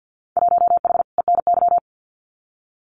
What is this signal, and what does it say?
Morse code, heard as a keyed tone, about 48 wpm, 700 Hz. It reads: 1H ENY
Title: Winter Field Day Exchanges 50wpm